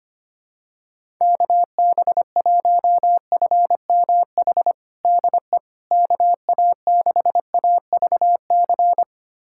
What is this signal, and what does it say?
Morse code, heard as a keyed tone, about 25 wpm, 700 Hz. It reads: KB1FMH DE KA6AVC
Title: USA Callsign DE Callsign - 6 Characters - 25wpm